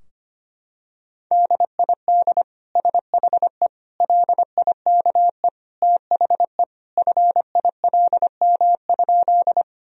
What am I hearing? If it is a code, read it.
Morse code, heard as a keyed tone, about 25 wpm, 700 Hz. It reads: DID SHE LIKE THE FILM?
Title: Sentences from Top 1000 Words - Review Part 3 - 25wpm